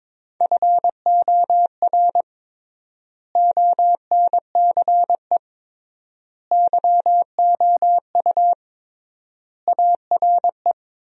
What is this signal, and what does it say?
Morse code, heard as a keyed tone, about 22 wpm, 700 Hz. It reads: FOR ONCE YOU ARE
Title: 3x Word-Spacing - Sentences from Top 300 Words - 22wpm